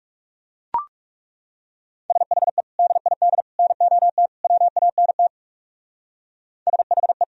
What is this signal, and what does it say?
Morse code, heard as a keyed tone, about 38 wpm, 700 Hz. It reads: SHE DID NOT WANT SHE